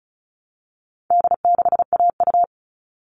Morse code, heard as a keyed tone, about 34 wpm, 700 Hz. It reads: D6AU
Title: All Cut Numbers - 4-Digit 35wpm